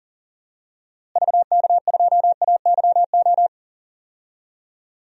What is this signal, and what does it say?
Morse code, heard as a keyed tone, about 38 wpm, 700 Hz. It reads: VK2AYO
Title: DXCC-100 Call Signs - 6 Characters - Repeated 3x - 40wpm